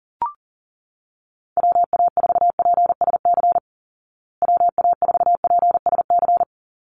Morse code, heard as a keyed tone, about 40 wpm, 700 Hz. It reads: WA4PSC WA4PSC
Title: Call Signs Repeated 2x - 40wpm